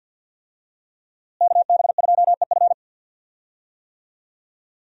Morse code, heard as a keyed tone, about 47 wpm, 700 Hz. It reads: KB2EF
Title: USA Call Signs - 5 Characters - Repeated 2x - 50wpm